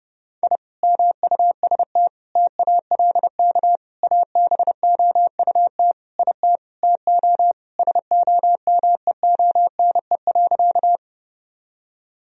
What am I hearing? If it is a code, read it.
Morse code, heard as a keyed tone, about 30 wpm, 700 Hz. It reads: I MUST TALK ABOUT IT TO SOMEONE.
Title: Sentences from Top 1000 Words - Review Part 4 - 30wpm